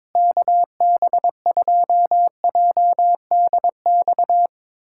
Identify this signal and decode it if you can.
Morse code, heard as a keyed tone, about 22 wpm, 700 Hz. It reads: KB2JDX
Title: USA Callsign DE Callsign 22wpm